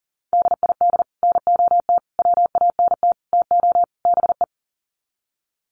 Morse code, heard as a keyed tone, about 40 wpm, 700 Hz. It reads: DID NOT WANT TO BE
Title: Sets of 5 Words Encore 40wpm